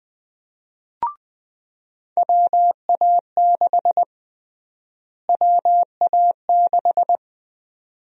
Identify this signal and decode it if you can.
Morse code, heard as a keyed tone, about 21 wpm, 700 Hz. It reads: WA6 WA6
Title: USA Call Sign Prefixes - Repeated 2x - 20wpm